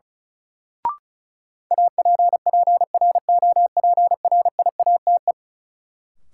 Morse code, heard as a keyed tone, about 36 wpm, 700 Hz. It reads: APPROPRIATE
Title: Sentences from Top 900 Words - New Word Drill - 35wpm